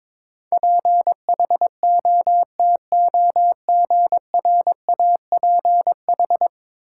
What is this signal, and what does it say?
Morse code, heard as a keyed tone, about 22 wpm, 700 Hz. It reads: PHOTOGRAPH
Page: morse-code-ninja-practice-22wpm